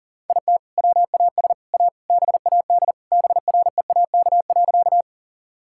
Morse code, heard as a keyed tone, about 40 wpm, 700 Hz. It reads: IT WAS A BAD BREAK.
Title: Sentences from Top 1000 Words - Review Part 6 - 40wpm